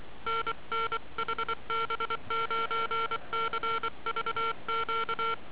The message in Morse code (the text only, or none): NN HB9CVQ